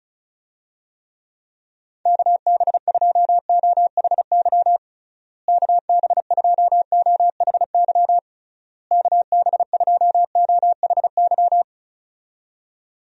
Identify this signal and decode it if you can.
Morse code, heard as a keyed tone, about 34 wpm, 700 Hz. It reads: KB2OHY KB2OHY KB2OHY
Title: USA Call Signs - 6 Characters - Repeated 3x - 35wpm